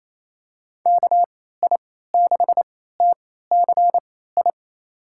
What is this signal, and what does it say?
Morse code, heard as a keyed tone, about 28 wpm, spaced as for 16 wpm, 700 Hz. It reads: KI6TCI
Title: USA Call Signs - 6 Characters - 3x Character Spacing - 28wpm